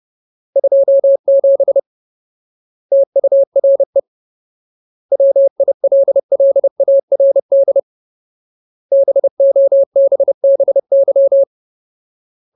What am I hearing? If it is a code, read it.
Morse code, heard as a keyed tone, about 30 wpm, 550 Hz. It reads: 27 TURE WILLARD BOBBY
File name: Ultimate Word List / 30WPM Ultimate